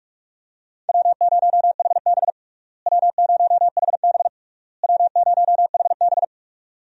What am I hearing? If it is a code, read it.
Morse code, heard as a keyed tone, about 41 wpm, 700 Hz. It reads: W0HB W0HB W0HB
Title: USA Call Signs - 4 Characters - Repeated 3x - 45wpm